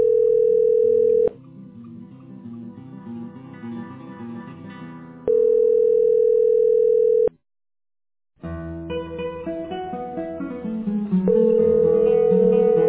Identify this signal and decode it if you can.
Ringback tone (North America), 440 + 480 Hz.